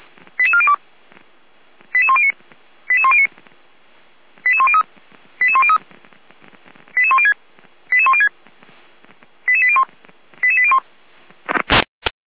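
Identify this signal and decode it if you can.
Five-tone selective call (ZVEI), heard as quick runs of five tones, tones of about 70 ms. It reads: 88402 88109 88109 88103 88103 88106 88106 88801 88801
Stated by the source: ZVEI five-tone selective call received via SDR.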